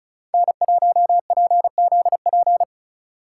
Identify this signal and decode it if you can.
Morse code, heard as a keyed tone, about 34 wpm, 700 Hz. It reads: N1PZP
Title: USA Call Signs - 5 Characters - 2x Character Spacing - 35wpm